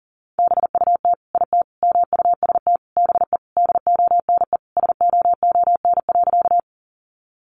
Morse code, heard as a keyed tone, about 40 wpm, 700 Hz. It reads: BUT IT MUST BE DONE SOON.
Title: Sentences from Top 500 Words 40wpm